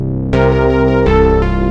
There is no code.